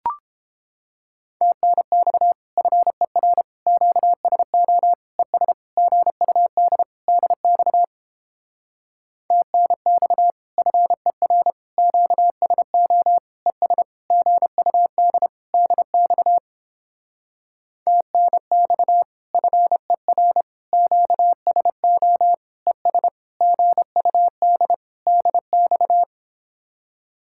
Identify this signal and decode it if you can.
Morse code, heard as a keyed tone, about 29 wpm, 700 Hz. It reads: TNX FER QSO ES GUD DX TNX FER QSO ES GUD DX TNX FER QSO ES GUD DX